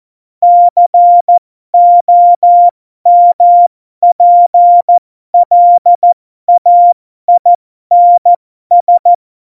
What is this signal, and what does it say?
Morse code, heard as a keyed tone, about 14 wpm, spaced as for 12 wpm, 700 Hz. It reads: COMPLAINS
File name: Morsekurs / Witze